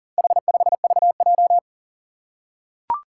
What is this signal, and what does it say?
Morse code, heard as a keyed tone, about 37 wpm, 700 Hz. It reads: H5VJ
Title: Four Challenging Letter-Number Combinations 40wpm